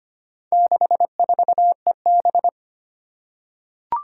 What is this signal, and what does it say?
Morse code, heard as a keyed tone, about 25 wpm, 700 Hz. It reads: 64EB
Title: Four Challenging Letter-Number Combinations 25wpm